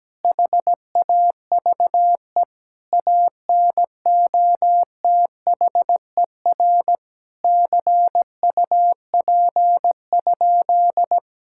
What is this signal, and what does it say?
Morse code, heard as a keyed tone, about 17 wpm, 700 Hz. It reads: HAVE ANOTHER CUP?
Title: Sentences from Top 1600 Words 17wpm